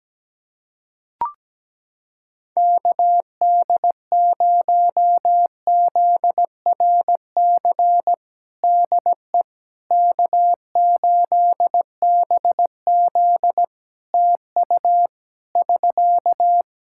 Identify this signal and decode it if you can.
Morse code, heard as a keyed tone, about 17 wpm, 700 Hz. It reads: KD0ZRC DE K8BZ TU <SK>
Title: Practice QSO - Exchange 4 - Ending - 17wpm